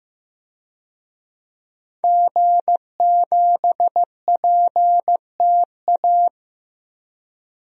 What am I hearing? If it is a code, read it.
Morse code, heard as a keyed tone, about 15 wpm, 700 Hz. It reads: G7PTA
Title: DXCC-100 Call Signs - 5 Characters - 15wpm